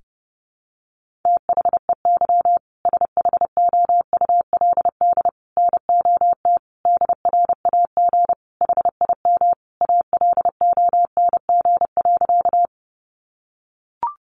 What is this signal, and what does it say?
Morse code, heard as a keyed tone, about 30 wpm, 700 Hz. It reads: THEY SHOULD NOT DRAG HIM ALONG.
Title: Sentences from Top 1500 Words 30wpm